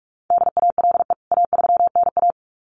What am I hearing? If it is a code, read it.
Morse code, heard as a keyed tone, about 44 wpm, 700 Hz. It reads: DALE A3NA
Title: CWT Exchange 45wpm